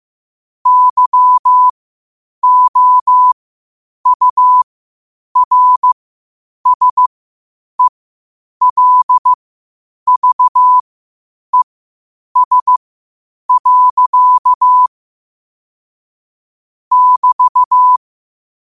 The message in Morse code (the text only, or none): YOURSELVES. <BT>